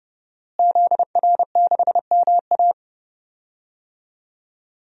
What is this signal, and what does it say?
Morse code, heard as a keyed tone, about 30 wpm, 700 Hz. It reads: ZR6MA
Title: DXCC-100 Call Signs - 5 Characters - Repeated 3x - 30wpm